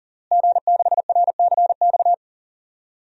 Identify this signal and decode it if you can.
Morse code, heard as a keyed tone, about 40 wpm, 700 Hz. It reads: G6RCX
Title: DXCC-100 Call Signs - 5 Characters - Repeated 3x - 40wpm